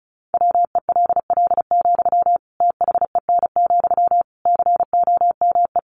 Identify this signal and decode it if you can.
Morse code, heard as a keyed tone, about 34 wpm, 700 Hz. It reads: WELL, THEN, COME
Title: Sentences from Top 700 Words 35wpm